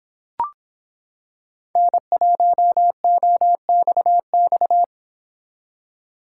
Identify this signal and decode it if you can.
Morse code, heard as a keyed tone, about 25 wpm, 700 Hz. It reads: N1OXX